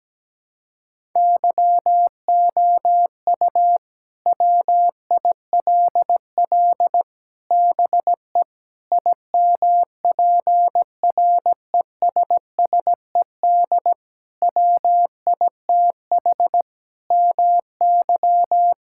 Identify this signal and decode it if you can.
Morse code, heard as a keyed tone, about 17 wpm, 700 Hz. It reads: YOU WILL BE IMPRESSED WITH MY
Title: Sentences from Top 1600 Words 17wpm